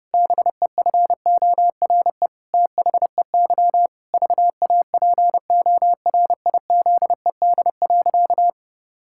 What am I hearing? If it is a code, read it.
Morse code, heard as a keyed tone, about 32 wpm, 700 Hz. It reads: BEFORE THEY VAPORIZED.
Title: Pangram Challenge v2 - 30wpm